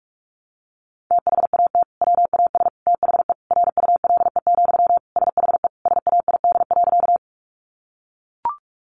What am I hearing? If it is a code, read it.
Morse code, heard as a keyed tone, about 44 wpm, 700 Hz. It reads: THAT WAS THE RULE, SHE SAID.
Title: Sentences from Top 400 Words 45wpm